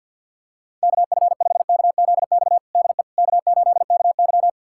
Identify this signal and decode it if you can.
Morse code, heard as a keyed tone, about 52 wpm, 700 Hz. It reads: KF5KZX DE K8KY
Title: USA Callsign DE Callsign 50wpm